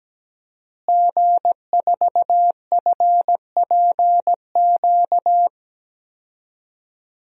Morse code, heard as a keyed tone, about 17 wpm, 700 Hz. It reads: G4FPQ
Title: DXCC-100 Call Signs - 5 Characters - Repeated 3x - 17wpm